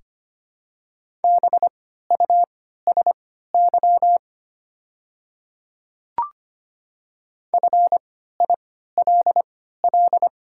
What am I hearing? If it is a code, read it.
Morse code, heard as a keyed tone, about 25 wpm, spaced as for 14 wpm, 700 Hz. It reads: BUSY FILL
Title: 3x Character Spacing - Top 500 Words - 25wpm